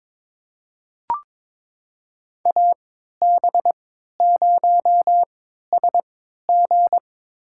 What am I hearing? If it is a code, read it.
Morse code, heard as a keyed tone, about 22 wpm, spaced as for 13 wpm, 700 Hz. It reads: AB0SG
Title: USA Call Signs - 5 Characters - 3x Character Spacing - 22wpm